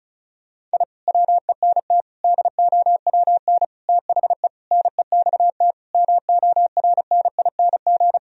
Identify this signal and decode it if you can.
Morse code, heard as a keyed tone, about 36 wpm, 700 Hz. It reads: I WENT DOWN THE NEXT MORNING
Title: Sentences from Top 1000 Words - Review Part 1 - 35wpm